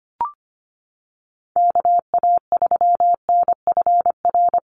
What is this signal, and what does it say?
Morse code, heard as a keyed tone, about 25 wpm, 700 Hz. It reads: KA3NFR
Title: USA Call Signs Repeated 3x 25wpm